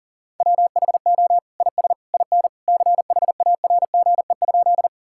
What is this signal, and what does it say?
Morse code, heard as a keyed tone, about 40 wpm, 700 Hz. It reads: WHO IS IN CHARGE?
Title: Sentences from Top 1000 Words - Review Part 6 - 40wpm